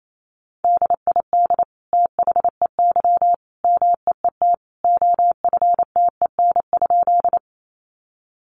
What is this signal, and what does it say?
Morse code, heard as a keyed tone, about 28 wpm, 700 Hz. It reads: DID THEY MEET OFTEN?
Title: Sentences from Top 1100 Words 28wpm